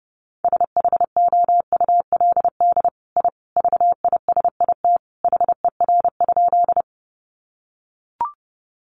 Morse code, heard as a keyed tone, about 30 wpm, 700 Hz. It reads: SHOULD I VISIT HER?
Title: Sentences from Top 900 Words 30wpm